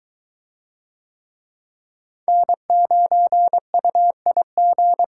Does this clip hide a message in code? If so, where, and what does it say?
Yes: Morse code from 2.3 to 5.0 s, heard as a keyed tone, about 22 wpm, 700 Hz. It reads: N9UIG